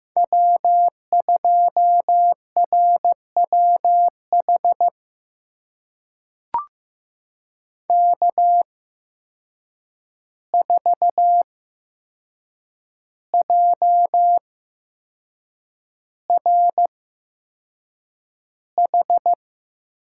Morse code, heard as a keyed tone, about 15 wpm, spaced as for 6 wpm, 700 Hz. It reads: W2RWH K4JRH
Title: USA Call Signs - 5 Characters - 8x Character Spacing - 15wpm